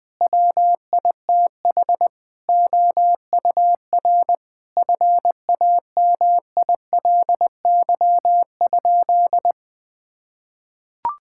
Morse code, heard as a keyed tone, about 21 wpm, 700 Hz. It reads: WITH OUR FAMILY?
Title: Sentences from Top 1000 Words - Review Part 6 - 20wpm